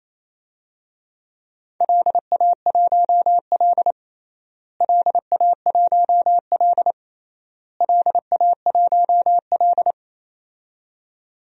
Morse code, heard as a keyed tone, about 28 wpm, 700 Hz. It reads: LA1L LA1L LA1L
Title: DXCC-100 Call Signs - 4 Characters - Repeated 3x - 28wpm